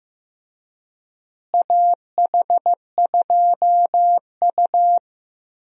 Morse code, heard as a keyed tone, about 15 wpm, 700 Hz. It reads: AH2U
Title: USA Call Signs - 4 Characters - Repeated 3x - 15wpm